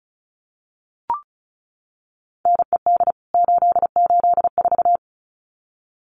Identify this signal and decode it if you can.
Morse code, heard as a keyed tone, about 34 wpm, 700 Hz. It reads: NED 884
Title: CWT Exchange 35wpm